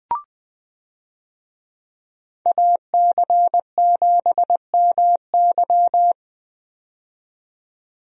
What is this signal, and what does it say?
Morse code, heard as a keyed tone, about 20 wpm, 700 Hz. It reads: AC7MY
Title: USA Call Signs - 5 Characters - 20wpm